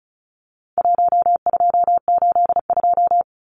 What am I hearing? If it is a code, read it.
Morse code, heard as a keyed tone, about 34 wpm, 700 Hz. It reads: 1282
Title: Numbers - 4-Digit 35wpm